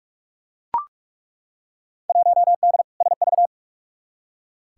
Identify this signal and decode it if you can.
Morse code, heard as a keyed tone, about 43 wpm, 700 Hz. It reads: 1D SV
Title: Field Day Exchanges - 45wpm